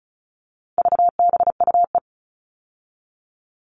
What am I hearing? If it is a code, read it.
Morse code, heard as a keyed tone, about 34 wpm, 700 Hz. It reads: VBUE
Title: All Cut Numbers - 4-Digit 35wpm